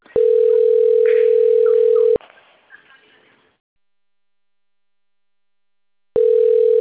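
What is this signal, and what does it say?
Ringback tone (North America), 440 + 480 Hz.
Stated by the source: Outbound telephone call recorded on the Cornelio Procopio system.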